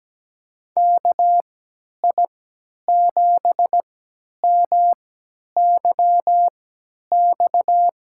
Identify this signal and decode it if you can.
Morse code, heard as a keyed tone, about 17 wpm, spaced as for 10 wpm, 700 Hz. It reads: KI7MYX
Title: USA Call Signs - 6 Characters - 3x Character Spacing - 17wpm